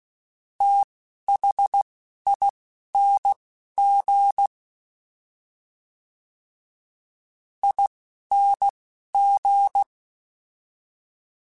Morse code, heard as a keyed tone, about 16 wpm, spaced as for 12 wpm, 800 Hz. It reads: THING ING
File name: Special INGs / Special INGs-mp3s / Fast ing @10